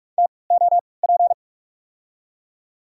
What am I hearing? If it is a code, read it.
Morse code, heard as a keyed tone, about 46 wpm, spaced as for 25 wpm, 700 Hz. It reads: TOP
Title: 3x Character Spacing - Top 500 Words - 45wpm